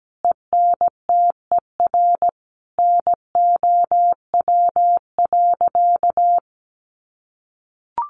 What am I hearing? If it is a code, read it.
Morse code, heard as a keyed tone, about 17 wpm, 700 Hz. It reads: ENTER NOW.
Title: Sentences from Top 1300 Words 17wpm